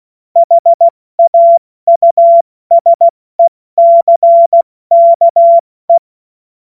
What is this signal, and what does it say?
Morse code, heard as a keyed tone, about 16 wpm, 680 Hz. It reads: HAUSECKE
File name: Witz-15-Juli-2025-70-F80